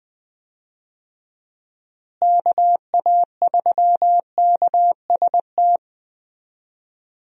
Morse code, heard as a keyed tone, about 20 wpm, 700 Hz. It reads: KA3KST
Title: USA Call Signs - 6 Characters - Repeated 3x - 20wpm